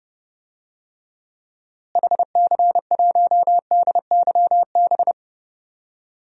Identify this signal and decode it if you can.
Morse code, heard as a keyed tone, about 30 wpm, 700 Hz. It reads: HC1DYB
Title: DXCC-100 Call Signs - 6 Characters - Repeated 2x - 30wpm